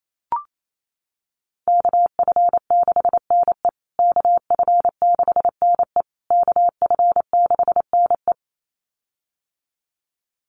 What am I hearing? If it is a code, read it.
Morse code, heard as a keyed tone, about 28 wpm, 700 Hz. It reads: KF6NE KF6NE KF6NE
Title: USA Call Signs Repeated 3x 28wpm